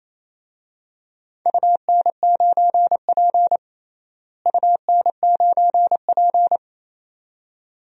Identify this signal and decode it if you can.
Morse code, heard as a keyed tone, about 28 wpm, 700 Hz. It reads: UN9P UN9P
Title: DXCC-100 Call Signs - 4 Characters - Repeated 2x - 28wpm